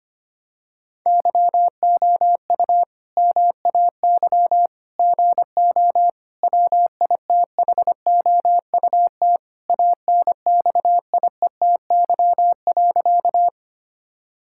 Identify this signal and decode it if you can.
Morse code, heard as a keyed tone, about 25 wpm, 700 Hz. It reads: YOU MAY GO WITHOUT ANXIETY.
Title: Sentences from Top 1600 Words 25wpm